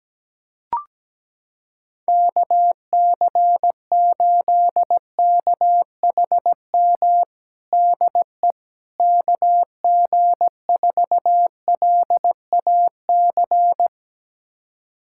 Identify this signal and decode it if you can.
Morse code, heard as a keyed tone, about 17 wpm, 700 Hz. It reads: KC8KHM DE KG4LAC
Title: USA Callsign DE Callsign - 6 Characters - 17wpm